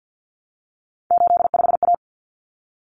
Morse code, heard as a keyed tone, about 48 wpm, 700 Hz. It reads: 85A
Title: Common Cut Numbers - 3-Digit 50wpm